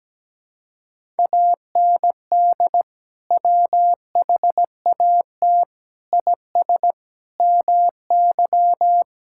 Morse code, heard as a keyed tone, about 17 wpm, 700 Hz. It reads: AND WHAT IS MY
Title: Sentences from Top 1000 Words - Review Part 6 - 17wpm